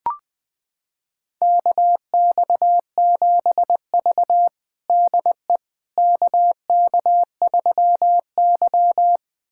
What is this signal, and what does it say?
Morse code, heard as a keyed tone, about 21 wpm, 700 Hz. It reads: KX7V DE KK3Y
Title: USA Callsign DE Callsign - 4 Characters - 20wpm